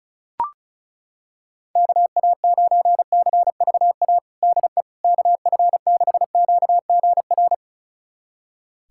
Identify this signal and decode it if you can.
Morse code, heard as a keyed tone, about 34 wpm, 700 Hz. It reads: KA9CVA DE KF6QGR
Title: USA Callsign DE Callsign 35wpm